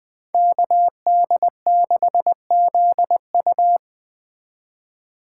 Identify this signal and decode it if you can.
Morse code, heard as a keyed tone, about 20 wpm, 700 Hz. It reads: KD6ZU
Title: Speed Racing - USA 2x2 Call Signs 17wpm